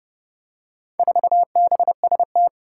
Morse code, heard as a keyed tone, about 32 wpm, 700 Hz. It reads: 4BST
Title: Four Challenging Letter-Number Combinations 30wpm